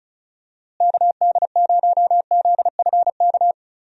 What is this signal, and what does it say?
Morse code, heard as a keyed tone, about 34 wpm, 700 Hz. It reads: KD0ZFK
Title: USA Call Signs - 6 Characters - Repeated 3x - 35wpm